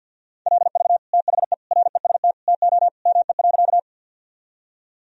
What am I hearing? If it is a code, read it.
Morse code, heard as a keyed tone, about 47 wpm, 700 Hz. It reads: LV THE REST TO ME.
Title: Sentences from Top 500 Words and Abbreviations 50wpm